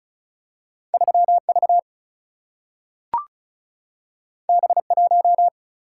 Morse code, heard as a keyed tone, about 36 wpm, 700 Hz. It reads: 3 V B 1
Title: Single-Letter Single-Number 35wpm